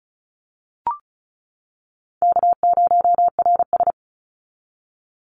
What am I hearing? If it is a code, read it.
Morse code, heard as a keyed tone, about 34 wpm, 700 Hz. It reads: K0RS
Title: NA SOTA Chaser Callsigns 35wpm